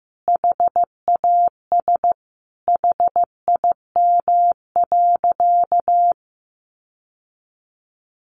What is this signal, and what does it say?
Morse code, heard as a keyed tone, about 14 wpm, 700 Hz. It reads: HAS HIM.
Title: Sentences from Top 800 Words 15wpm